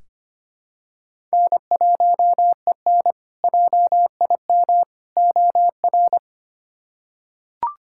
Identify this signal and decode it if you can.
Morse code, heard as a keyed tone, about 25 wpm, 700 Hz. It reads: N1EN JIM OR
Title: K1USN SST QSO Practice - 25wpm